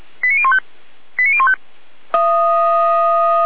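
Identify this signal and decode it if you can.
Five-tone selective call (ZVEI), heard as quick runs of five tones, tones of about 80 ms. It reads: 89015 89015
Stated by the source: ZVEI five-tone selective call 89050, received via SDR.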